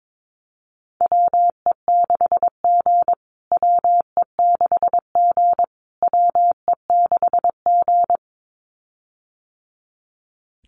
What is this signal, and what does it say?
Morse code, heard as a keyed tone, about 22 wpm, 700 Hz. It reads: WE6G WE6G WE6G
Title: USA Call Signs Repeated 3x 22wpm